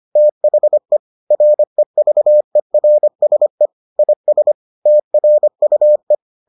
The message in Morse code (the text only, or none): THE REVERSE IS TRUE